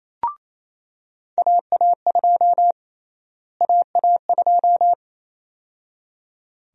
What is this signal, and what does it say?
Morse code, heard as a keyed tone, about 28 wpm, 700 Hz. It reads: AA2 AA2
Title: USA Call Sign Prefixes - Repeated 2x - 28wpm